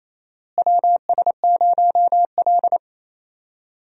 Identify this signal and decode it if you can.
Morse code, heard as a keyed tone, about 28 wpm, 700 Hz. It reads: WS0L
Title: USA Call Signs - 4 Characters - Repeated 3x - 28wpm